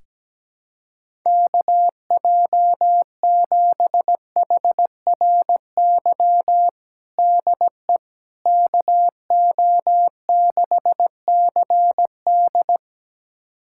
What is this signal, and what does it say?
Morse code, heard as a keyed tone, about 17 wpm, 700 Hz. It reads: KJ7HRY DE KO6CD
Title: USA Callsign DE Callsign 17wpm